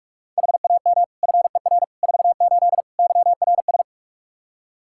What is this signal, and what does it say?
Morse code, heard as a keyed tone, about 46 wpm, 700 Hz. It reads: HAM FER 48 YRS
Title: K1USN SST QSO Practice 45wpm